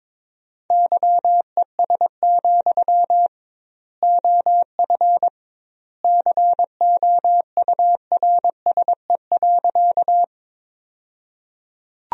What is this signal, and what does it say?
Morse code, heard as a keyed tone, about 22 wpm, 700 Hz. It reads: YES, OF COURSE.
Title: 2x Word-Spacing - Sentences from Top 500 Words - 22wpm